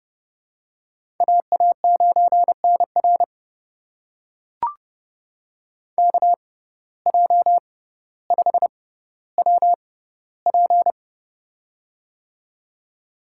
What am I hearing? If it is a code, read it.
Morse code, heard as a keyed tone, about 30 wpm, spaced as for 15 wpm, 700 Hz. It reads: AA9NR KJ5WP
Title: USA Call Signs - 5 Characters - 6x Character Spacing - 30wpm